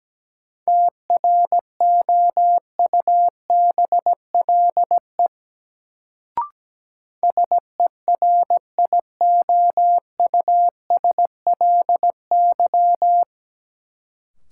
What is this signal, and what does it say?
Morse code, heard as a keyed tone, about 17 wpm, 700 Hz. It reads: TROUBLE SERIOUSLY
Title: Sentences from Top 1200 Words - New Word Drill - 17wpm